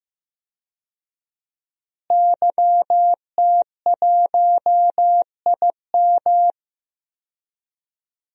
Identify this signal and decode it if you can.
Morse code, heard as a keyed tone, about 15 wpm, 700 Hz. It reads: YT1IM
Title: DXCC-100 Call Signs - 5 Characters - Repeated 3x - 15wpm